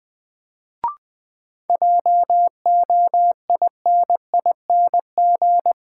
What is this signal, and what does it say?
Morse code, heard as a keyed tone, about 20 wpm, 700 Hz. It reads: JOINING
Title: Sets of 5 Words - New Word Drill - 20wpm